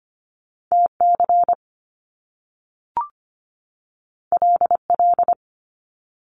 Morse code, heard as a keyed tone, about 25 wpm, 700 Hz. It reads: T C L L
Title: Top Two Letter Combinations in Words 25wpm